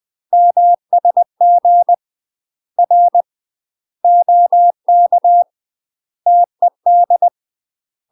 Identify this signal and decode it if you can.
Morse code, heard as a keyed tone, about 20 wpm, 700 Hz. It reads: MSG R OK TED
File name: UK MASTER QSOs TRAINING / 20WPM / UK MASTER QSOs 0041